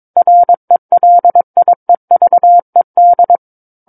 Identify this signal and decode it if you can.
Morse code, heard as a keyed tone, about 22 wpm, 700 Hz. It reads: RELIEVED
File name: 06-Chapter-06 22 WPM